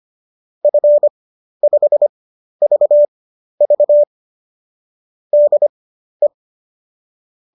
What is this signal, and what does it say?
Morse code, heard as a keyed tone, about 25 wpm, spaced as for 12 wpm, 590 Hz. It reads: F5VV DE
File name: qso009 12